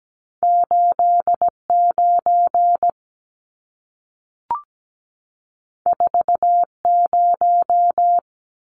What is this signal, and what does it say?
Morse code, heard as a keyed tone, about 17 wpm, 700 Hz. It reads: 89 40
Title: Numbers - 2-Digit 17wpm